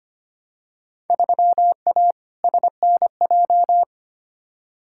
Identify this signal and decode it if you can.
Morse code, heard as a keyed tone, about 25 wpm, 700 Hz. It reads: 3A SNJ
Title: Field Day Exchanges - 25wpm